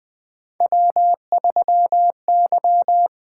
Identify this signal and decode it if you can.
Morse code, heard as a keyed tone, about 20 wpm, 700 Hz. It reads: W3Y
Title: USA Call Signs - 4 Characters - 8x Character Spacing - 20wpm